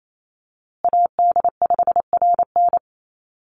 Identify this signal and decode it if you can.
Morse code, heard as a keyed tone, about 28 wpm, 700 Hz. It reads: AD5RN
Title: USA Call Signs Repeated 2x 28wpm